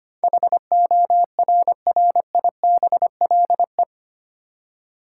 Morse code, heard as a keyed tone, about 25 wpm, 700 Hz. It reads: HORRIBLE
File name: 8 Letter Words - First 3 Letters Given - 25wpm